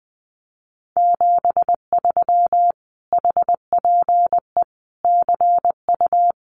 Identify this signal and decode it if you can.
Morse code, heard as a keyed tone, about 20 wpm, 700 Hz. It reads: 73 HPE CU
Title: Common QSO Phrases Encore 20wpm